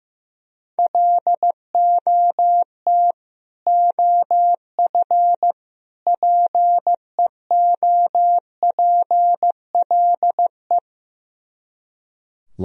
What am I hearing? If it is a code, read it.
Morse code, heard as a keyed tone, about 15 wpm, 700 Hz. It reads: LOT OF PEOPLE
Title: Speed-Racing - Sets of 3 Words 15wpm